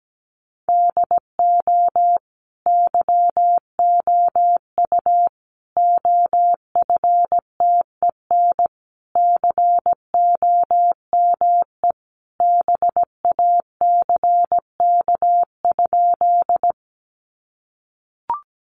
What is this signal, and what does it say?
Morse code, heard as a keyed tone, about 17 wpm, 700 Hz. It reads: DO YOU OFTEN COME BACK?
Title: Sentences from Top 300 Words 17wpm